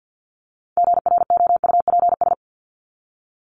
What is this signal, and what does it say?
Morse code, heard as a keyed tone, about 50 wpm, 700 Hz. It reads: GROUPS
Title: ES:S Suffix - Top 100 Words with Base Words 50wpm